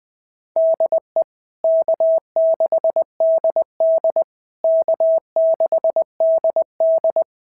Morse code, heard as a keyed tone, about 20 wpm, 650 Hz. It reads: DE K6DD K6DD
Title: SOTA QSO Practice 20wpm